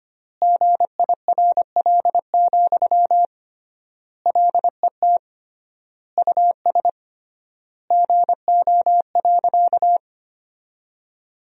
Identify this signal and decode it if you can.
Morse code, heard as a keyed tone, about 25 wpm, 700 Hz. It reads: GIRL, LET US GO.
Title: 3x Word-Spacing - Sentences from Top 400 Words - 25wpm